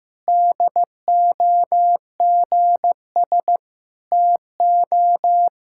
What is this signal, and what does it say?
Morse code, heard as a keyed tone, about 16 wpm, 700 Hz. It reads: DOGS TO
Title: Pangram Challenge v2 - 15wpm